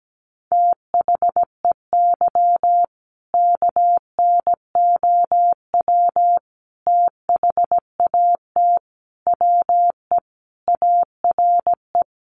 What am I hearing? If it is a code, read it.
Morse code, heard as a keyed tone, about 17 wpm, 700 Hz. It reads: THEY KNOW THAT WE ARE
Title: Sentences from Top 1400 Words 17wpm